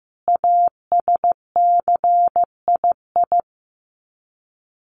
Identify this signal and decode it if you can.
Morse code, heard as a keyed tone, about 15 wpm, 700 Hz. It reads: ASCII
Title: Sets of 5 Words 15wpm